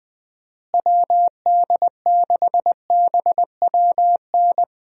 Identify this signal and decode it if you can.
Morse code, heard as a keyed tone, about 20 wpm, 700 Hz. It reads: WD6BWN
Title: USA Call Signs - 6 Characters - Repeated 2x - 20wpm